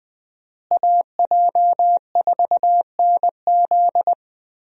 Morse code, heard as a keyed tone, about 20 wpm, 700 Hz. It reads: AJ4NZ
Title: USA Call Signs - 5 Characters - Repeated 2x - 20wpm